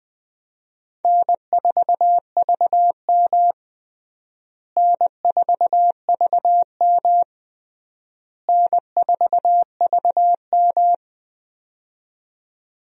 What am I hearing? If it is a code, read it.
Morse code, heard as a keyed tone, about 20 wpm, 700 Hz. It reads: N4VM N4VM N4VM
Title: USA Call Signs - 4 Characters - Repeated 3x - 20wpm